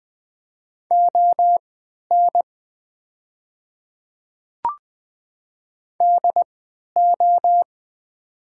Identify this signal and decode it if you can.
Morse code, heard as a keyed tone, about 20 wpm, spaced as for 11 wpm, 700 Hz. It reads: ON DO
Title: 3x Character Spacing - Top 200 Words - 20wpm